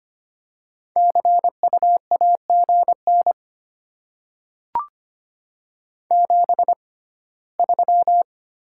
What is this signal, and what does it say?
Morse code, heard as a keyed tone, about 25 wpm, spaced as for 17 wpm, 700 Hz. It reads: CU AGN 73
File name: Top 20 QSO Elements - 6x Character Spacing - 25wpm